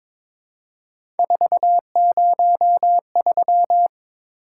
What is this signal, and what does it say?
Morse code, heard as a keyed tone, about 22 wpm, 700 Hz. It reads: 403
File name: practice / 22wpm / Three Letter-Number Combinations - 22wpm